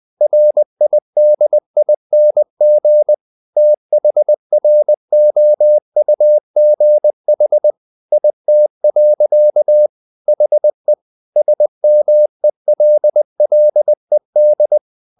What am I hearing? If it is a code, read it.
Morse code, heard as a keyed tone, about 20 wpm, 600 Hz. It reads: RIDING THROUGH IT. HE SMELLED
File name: the old man and the sea / the old man and the sea part 1 20wpm